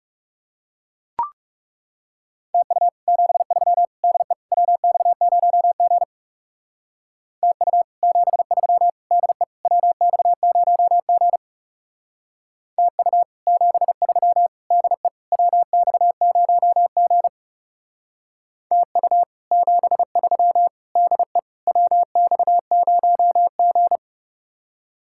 Morse code, heard as a keyed tone, about 36 wpm, 700 Hz. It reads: TU 73 DE WX0G TU 73 DE WX0G TU 73 DE WX0G TU 73 DE WX0G
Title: Speed-Racing Common QSO Phrases - 30wpm